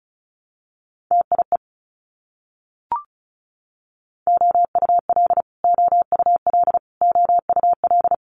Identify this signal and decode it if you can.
Morse code, heard as a keyed tone, about 36 wpm, 700 Hz. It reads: TIE OUL OUL OUL
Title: Multi-letter Phonemes Repeated 3x 35wpm